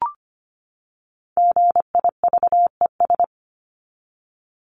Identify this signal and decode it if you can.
Morse code, heard as a keyed tone, about 25 wpm, 700 Hz. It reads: GIVES
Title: ES:S Suffix - Top 100 Words with Base Words 25wpm